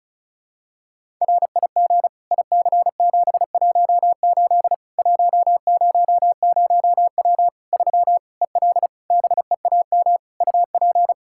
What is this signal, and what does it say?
Morse code, heard as a keyed tone, about 36 wpm, 700 Hz. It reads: RIG IC718 100W 3 EL BEAM UP
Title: Practice QSO - Exchange 2 - Info - 35wpm